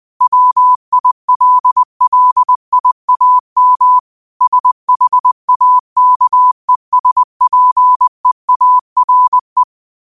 Morse code, heard as a keyed tone, about 20 wpm, 1000 Hz. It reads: WILLIAM SHAKESPEARE